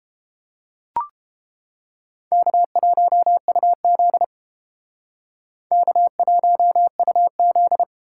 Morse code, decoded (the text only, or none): K1UZ K1UZ